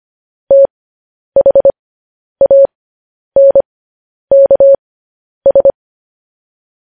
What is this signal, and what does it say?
Morse code, heard as a keyed tone, about 25 wpm, spaced as for 10 wpm, 550 Hz. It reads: THANKS